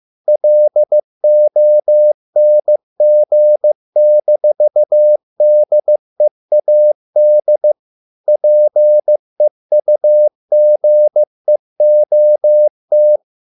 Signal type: Morse code